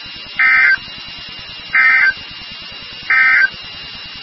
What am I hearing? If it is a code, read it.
Emergency Alert System (SAME) tones, heard as harsh buzzing bursts. It is heard from 0.3 to 3.6 s.
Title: NOAA Weather Radio WXK36